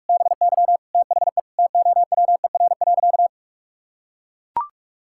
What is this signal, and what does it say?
Morse code, heard as a keyed tone, about 44 wpm, 700 Hz. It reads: BY THE TOWER.
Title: Pangram Challenge v2 - 45wpm